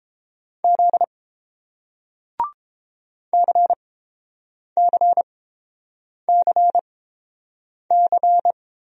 Morse code, heard as a keyed tone, about 30 wpm, spaced as for 7 wpm, 700 Hz. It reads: Z CCCC